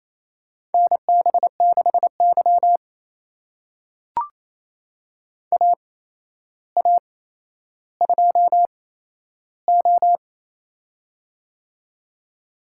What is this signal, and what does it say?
Morse code, heard as a keyed tone, about 28 wpm, spaced as for 12 wpm, 700 Hz. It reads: NB6Y AA2O
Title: USA Call Signs - 4 Characters - 8x Character Spacing - 28wpm